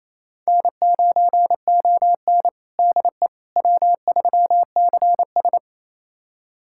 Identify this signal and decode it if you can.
Morse code, heard as a keyed tone, about 28 wpm, 700 Hz. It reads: N9ON DE W3CS
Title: USA Callsign DE Callsign - 4 Characters - 28wpm